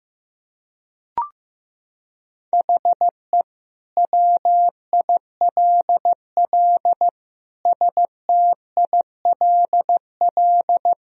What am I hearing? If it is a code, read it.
Morse code, heard as a keyed tone, about 15 wpm, 700 Hz. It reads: HE WILL STILL
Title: Sentences from Top 1000 Words - Review Part 1 - 15wpm